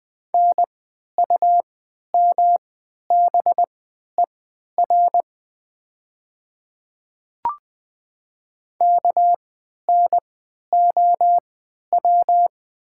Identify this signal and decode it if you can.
Morse code, heard as a keyed tone, about 20 wpm, spaced as for 11 wpm, 700 Hz. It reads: NUMBER KNOW
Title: 3x Character Spacing - Top 200 Words - 20wpm